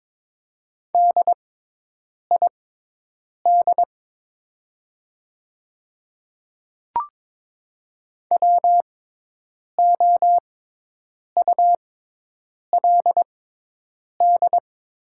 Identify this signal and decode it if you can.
Morse code, heard as a keyed tone, about 22 wpm, spaced as for 8 wpm, 700 Hz. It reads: DID WOULD
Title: 6x Character Spacing - Top 100 Words - 22wpm